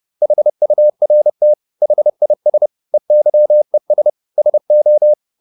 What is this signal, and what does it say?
Morse code, heard as a keyed tone, about 30 wpm, 600 Hz. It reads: HURT HIS EYES SO
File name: the old man and the sea / the old man and the sea 1 30wpm